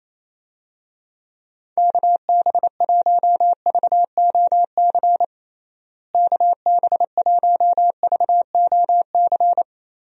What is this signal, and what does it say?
Morse code, heard as a keyed tone, about 28 wpm, 700 Hz. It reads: KB1VOC KB1VOC
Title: USA Call Signs - 6 Characters - Repeated 2x - 28wpm